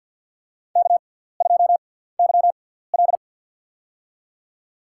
Morse code, heard as a keyed tone, about 45 wpm, spaced as for 17 wpm, 700 Hz. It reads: K2YL
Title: USA Call Signs - 4 Characters - 6x Character Spacing - 50wpm